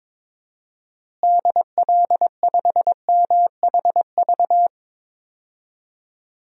Morse code, heard as a keyed tone, about 22 wpm, 700 Hz. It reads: DL5MHV
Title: DXCC-100 Call Signs - 6 Characters - Repeated 2x - 22wpm